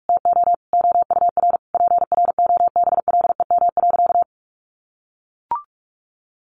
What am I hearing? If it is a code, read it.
Morse code, heard as a keyed tone, about 44 wpm, 700 Hz. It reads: TO OUR PROBLEM.
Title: Sentences from Top 1300 Words 45wpm